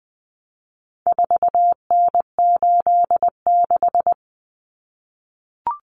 Morse code, heard as a keyed tone, about 20 wpm, 700 Hz. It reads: 4N86
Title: Common Cut Numbers - 4-Digit 20wpm